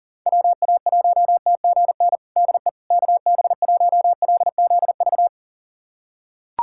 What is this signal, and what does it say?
Morse code, heard as a keyed tone, about 40 wpm, 700 Hz. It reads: WA1TGN DE KB1LZV
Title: USA Callsign DE Callsign - 6 Characters - 40wpm